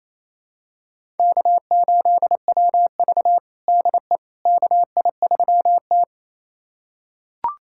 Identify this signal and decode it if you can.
Morse code, heard as a keyed tone, about 27 wpm, 700 Hz. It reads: K8WV DE KI3T
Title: USA Callsign DE Callsign - 4 Characters - 28wpm